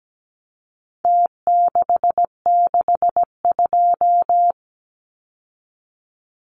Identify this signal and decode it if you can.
Morse code, heard as a keyed tone, about 17 wpm, 700 Hz. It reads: T662
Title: Numbers - Contest Style - 4-Digit 17wpm